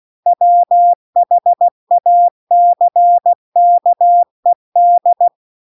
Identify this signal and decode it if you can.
Morse code, heard as a keyed tone, about 16 wpm, 700 Hz. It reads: WHACKED